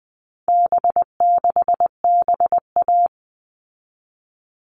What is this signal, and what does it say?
Morse code, heard as a keyed tone, about 20 wpm, 700 Hz. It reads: B6BA
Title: All Cut Numbers - 4-Digit 20wpm